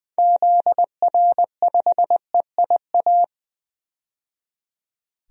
Morse code, heard as a keyed tone, about 21 wpm, 700 Hz. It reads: ZR5EIA
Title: DXCC-100 Call Signs - 6 Characters - 20wpm